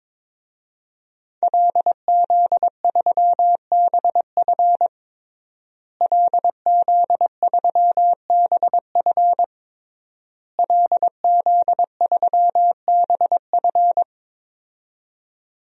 Morse code, heard as a keyed tone, about 22 wpm, 700 Hz. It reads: LZ3BF LZ3BF LZ3BF
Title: DXCC-100 Call Signs - 5 Characters - Repeated 3x - 22wpm